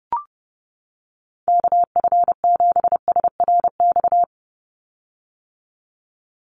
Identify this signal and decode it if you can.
Morse code, heard as a keyed tone, about 30 wpm, 700 Hz. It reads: KF7SRX
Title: USA Call Signs 30wpm